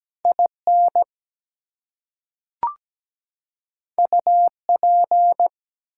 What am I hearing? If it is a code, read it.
Morse code, heard as a keyed tone, about 17 wpm, 700 Hz. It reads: IN UP
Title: Two Letter Words - 17wpm